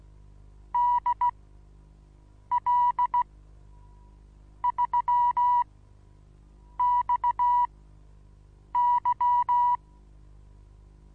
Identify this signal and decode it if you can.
Morse code, heard as a keyed tone, about 16 wpm, spaced as for 6 wpm, 970 Hz. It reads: DL3XY
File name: Onlinelehrgang / morsen / DJ4UF Morsen25